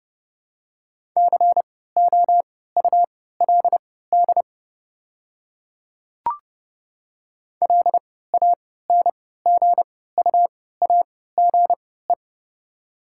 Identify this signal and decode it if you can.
Morse code, heard as a keyed tone, about 32 wpm, spaced as for 17 wpm, 700 Hz. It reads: COULD LANGUAGE
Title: 3x Character Spacing - Top 500 Words - 30wpm